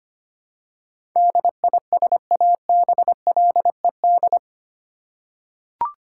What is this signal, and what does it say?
Morse code, heard as a keyed tone, about 25 wpm, 690 Hz. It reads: DISABLED
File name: 8 Letter Words - First 3 Letters Given - 25wpm